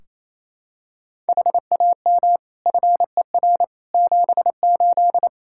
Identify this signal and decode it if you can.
Morse code, heard as a keyed tone, about 28 wpm, 700 Hz. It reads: HAM FER 78
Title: Practice QSO - Exchange 3 - More Info - 28wpm